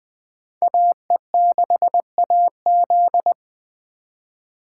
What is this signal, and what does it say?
Morse code, heard as a keyed tone, about 24 wpm, 700 Hz. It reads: AE6AZ
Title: Speed Racing - USA 2x2 Call Signs 20wpm